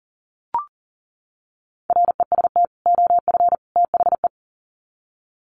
Morse code, heard as a keyed tone, about 37 wpm, 700 Hz. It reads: REST OF THE
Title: Sets of 3 Words 40wpm